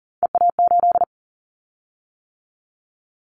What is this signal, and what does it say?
Morse code, heard as a keyed tone, about 40 wpm, 700 Hz. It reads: EA8
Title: Callsign Prefixes for Top 100 DXCC - 40wpm